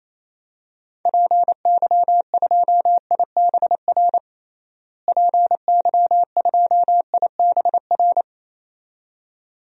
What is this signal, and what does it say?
Morse code, heard as a keyed tone, about 28 wpm, 700 Hz. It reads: PY2IBR PY2IBR
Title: DXCC-100 Call Signs - 6 Characters - Repeated 2x - 28wpm